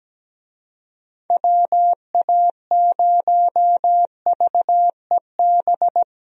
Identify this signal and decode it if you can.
Morse code, heard as a keyed tone, about 17 wpm, 700 Hz. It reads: WA0VEB
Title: USA Call Signs - 6 Characters - 17wpm